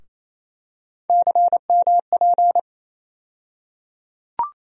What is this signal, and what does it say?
Morse code, heard as a keyed tone, about 27 wpm, 700 Hz. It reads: CMP
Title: USA Call Sign Postfixes - 3 Letters 28wpm